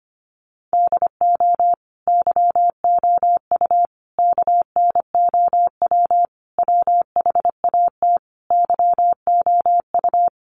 Morse code, heard as a keyed tone, about 25 wpm, 700 Hz. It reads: DO YOU KNOW WHAT YOU
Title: Sentences from Top 100 Words 25wpm